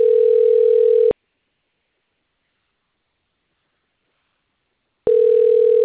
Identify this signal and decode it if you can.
Ringback tone (North America), 440 + 480 Hz.
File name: rg-1600-unknown- - - .13119